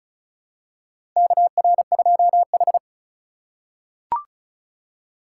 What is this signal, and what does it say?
Morse code, heard as a keyed tone, about 36 wpm, 700 Hz. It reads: KR2H
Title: USA Call Signs - 4 Characters - 2x Character Spacing - 35wpm